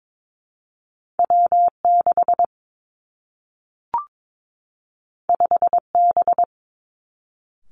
Morse code, heard as a keyed tone, about 22 wpm, 700 Hz. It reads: W 6 5 B
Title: Two Challenging Letter-Number Combinations 22wpm